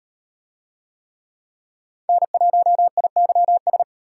Morse code, heard as a keyed tone, about 38 wpm, 700 Hz. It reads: N1IYS